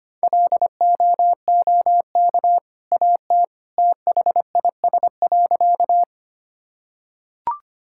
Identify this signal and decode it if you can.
Morse code, heard as a keyed tone, about 25 wpm, 700 Hz. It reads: LOOK AT THIS.
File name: practice / 25wpm / Sentences with 2-Word Phrases - 25wpm